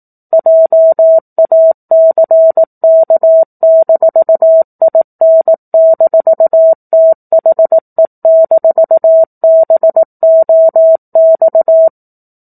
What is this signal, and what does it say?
Morse code, heard as a keyed tone, about 18 wpm, 650 Hz. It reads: JACK-IN-THE-BOX
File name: 04-Chapter-04 18 WPM